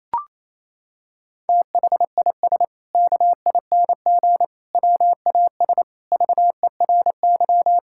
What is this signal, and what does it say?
Morse code, heard as a keyed tone, about 28 wpm, 700 Hz. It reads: THIS KING WAS VERY
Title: Sentences from Top 1600 Words 28wpm